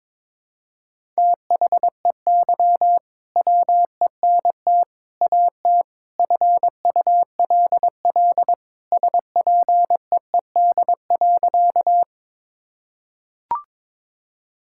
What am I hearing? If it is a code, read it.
Morse code, heard as a keyed tone, about 21 wpm, 700 Hz. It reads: THEY WENT AT FULL SPEED.
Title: Sentences from Top 1000 Words - Review Part 5 - 22wpm